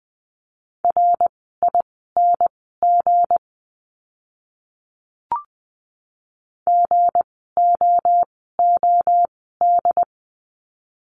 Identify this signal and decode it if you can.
Morse code, heard as a keyed tone, about 20 wpm, spaced as for 14 wpm, 700 Hz. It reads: RING GOOD
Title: 160 Lesson 26f - Letter g - Words 20wpm